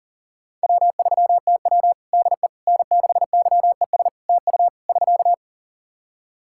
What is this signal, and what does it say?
Morse code, heard as a keyed tone, about 40 wpm, 700 Hz. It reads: W3TW DE N6YES TU <SK>
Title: Practice QSO - Exchange 4 - Ending - 40wpm